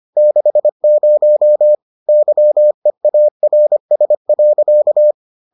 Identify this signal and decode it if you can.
Morse code, heard as a keyed tone, about 25 wpm, 600 Hz. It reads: 60 YEARS.